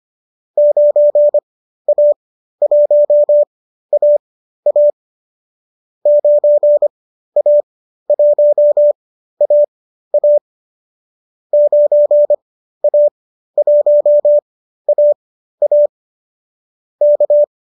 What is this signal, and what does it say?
Morse code, heard as a keyed tone, about 25 wpm, spaced as for 13 wpm, 590 Hz. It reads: 9A1AA 9A1AA 9A1AA K
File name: qso219 13